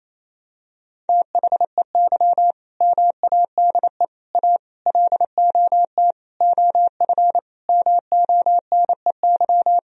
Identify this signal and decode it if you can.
Morse code, heard as a keyed tone, about 28 wpm, 700 Hz. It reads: THEY MADE A LOT OF MONEY
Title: Sentences from Top 1000 Words - Review Part 4 - 28wpm